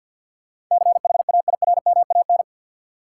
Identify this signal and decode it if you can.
Morse code, heard as a keyed tone, about 52 wpm, 700 Hz. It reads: CHAIRMAN
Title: Sets of 3 Words - New Word Drill - 50wpm